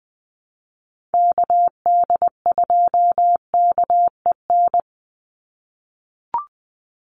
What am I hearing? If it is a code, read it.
Morse code, heard as a keyed tone, about 20 wpm, 700 Hz. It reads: KD2KEN
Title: USA Call Signs 20wpm